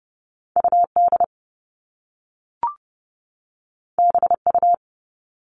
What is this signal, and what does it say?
Morse code, heard as a keyed tone, about 30 wpm, 700 Hz. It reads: UD BU
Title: All Cut Numbers - 2-Digit 30wpm